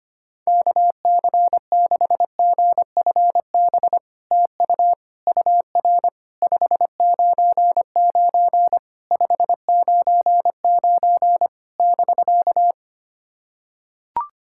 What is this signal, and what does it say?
Morse code, heard as a keyed tone, about 25 wpm, 700 Hz. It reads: KC6GFB TU UR 599 599 <BK>
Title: SOTA QSO Practice 25wpm